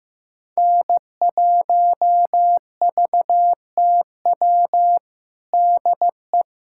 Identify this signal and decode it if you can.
Morse code, heard as a keyed tone, about 15 wpm, 700 Hz. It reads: N1VTW DE
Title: USA Callsign DE Callsign - 5 Characters - 15wpm